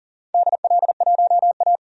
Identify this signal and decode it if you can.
Morse code, heard as a keyed tone, about 40 wpm, 690 Hz. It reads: DL1A
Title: DXCC-100 Call Signs - 4 Characters - 40wpm